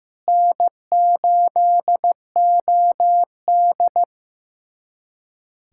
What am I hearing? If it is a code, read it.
Morse code, heard as a keyed tone, about 15 wpm, 700 Hz. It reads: N8OD
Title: USA Callsign DE Callsign - 4 Characters - 15wpm